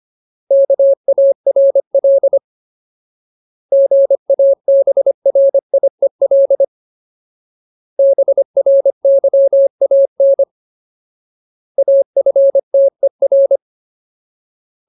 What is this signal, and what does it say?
Morse code, heard as a keyed tone, about 25 wpm, 550 Hz. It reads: KARL GABRIEL BRYAN AFTER
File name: Ultimate Word List / 25WPM Ultimate